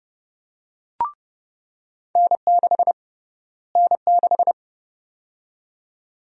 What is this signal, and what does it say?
Morse code, heard as a keyed tone, about 30 wpm, 700 Hz. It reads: N6 N6
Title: USA Call Sign Prefixes - Repeated 2x - 30wpm